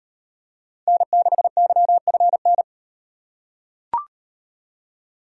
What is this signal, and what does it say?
Morse code, heard as a keyed tone, about 38 wpm, 700 Hz. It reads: N6YFN